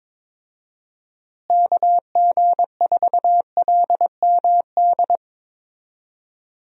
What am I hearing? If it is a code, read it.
Morse code, heard as a keyed tone, about 22 wpm, 700 Hz. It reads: KG4LMD
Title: USA Call Signs - 6 Characters - Repeated 2x - 22wpm